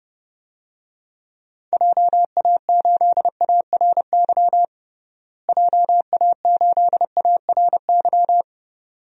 Morse code, heard as a keyed tone, about 30 wpm, 700 Hz. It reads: JA8ARY JA8ARY
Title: DXCC-100 Call Signs - 6 Characters - Repeated 2x - 30wpm